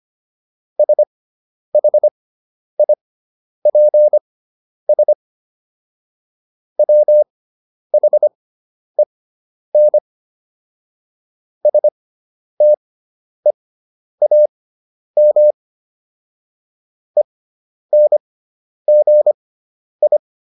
Morse code, heard as a keyed tone, about 25 wpm, spaced as for 10 wpm, 600 Hz. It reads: SHIPS WHEN STEAM ENGI
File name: SS213 10